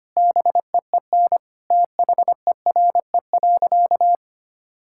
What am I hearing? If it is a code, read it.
Morse code, heard as a keyed tone, about 25 wpm, 700 Hz. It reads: BEEN THERE.
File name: practice / 25wpm / Sentences with 4-Word Phrases - 25wpm